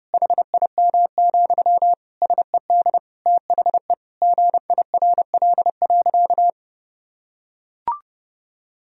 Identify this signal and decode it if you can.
Morse code, heard as a keyed tone, about 32 wpm, 700 Hz. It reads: HIM, SED THE GIRL.
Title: Sentences from Top 400 Words and Abbreviations 30wpm